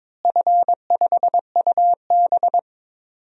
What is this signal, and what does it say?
Morse code, heard as a keyed tone, about 22 wpm, 700 Hz. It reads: F5UB
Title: DXCC-100 Call Signs - 4 Characters - Repeated 3x - 22wpm